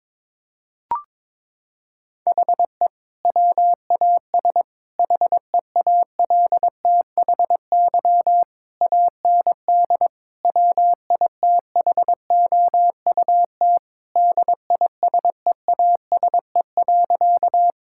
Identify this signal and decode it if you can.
Morse code, heard as a keyed tone, about 23 wpm, 700 Hz. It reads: HE WAS HEALTHY AND WITHOUT DISEASE.
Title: Sentences from Top 1000 Words - Review Part 6 - 22wpm